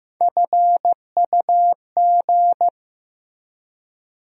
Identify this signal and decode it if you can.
Morse code, heard as a keyed tone, about 15 wpm, 700 Hz. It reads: FUG
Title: USA Call Sign Postfixes - 3 Letters 15wpm